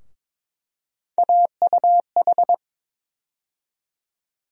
Morse code, heard as a keyed tone, about 22 wpm, 700 Hz. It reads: AUH
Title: USA Call Sign Postfixes - 3 Letters 22wpm